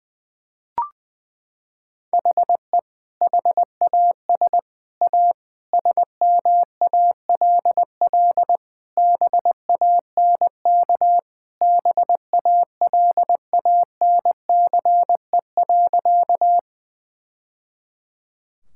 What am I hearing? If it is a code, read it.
Morse code, heard as a keyed tone, about 20 wpm, 700 Hz. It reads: HE HAS A SMALL BANK BALANCE.
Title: Sentences from Top 1000 Words - Review Part 6 - 20wpm